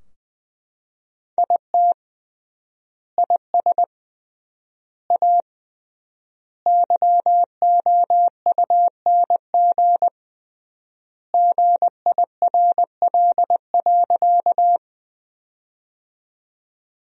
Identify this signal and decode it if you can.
Morse code, heard as a keyed tone, about 20 wpm, 700 Hz. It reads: IT IS A YOUNG GIRL.
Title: 3x Word-Spacing - Sentences from Top 400 Words - 20wpm